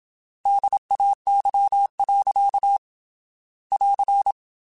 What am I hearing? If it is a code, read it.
Morse code, heard as a keyed tone, about 27 wpm, 800 Hz. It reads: DAY. <AR>
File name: Weather Sets / Weather Sets-mp3s / Weather 5 12 @25